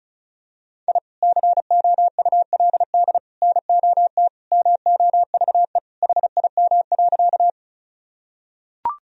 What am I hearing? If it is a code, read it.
Morse code, heard as a keyed tone, about 36 wpm, 700 Hz. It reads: I COULD NOT MOVE HIM.
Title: Sentences from Top 1000 Words - Review Part 1 - 35wpm